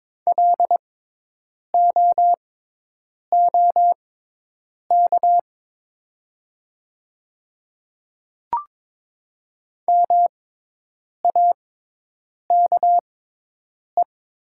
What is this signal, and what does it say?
Morse code, heard as a keyed tone, about 22 wpm, spaced as for 8 wpm, 700 Hz. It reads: LOOK MAKE
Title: 6x Character Spacing - Top 100 Words - 22wpm